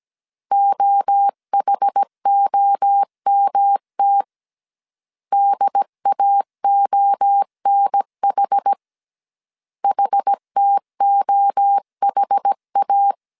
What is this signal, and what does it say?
Morse code, heard as a keyed tone, about 17 wpm, 800 Hz. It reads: OHOMT DAONH HTOHA
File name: CW / DET / 1 3 1500C S A 16wpm 2141